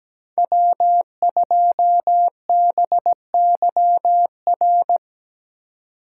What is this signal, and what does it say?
Morse code, heard as a keyed tone, about 20 wpm, 700 Hz. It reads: W2BYR
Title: Speed Racing - USA 1x3 Call Signs 17wpm